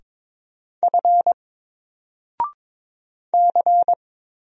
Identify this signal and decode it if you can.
Morse code, heard as a keyed tone, about 22 wpm, 700 Hz. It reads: F C
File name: Single-Letter-Number-22wpm